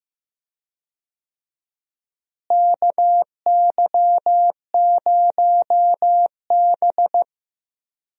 Morse code, heard as a keyed tone, about 15 wpm, 700 Hz. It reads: KY0B
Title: USA Call Signs - 4 Characters - Repeated 3x - 15wpm